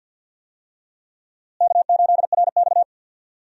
Morse code, heard as a keyed tone, about 48 wpm, 700 Hz. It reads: K8RX
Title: USA Call Signs - 4 Characters - Repeated 2x - 50wpm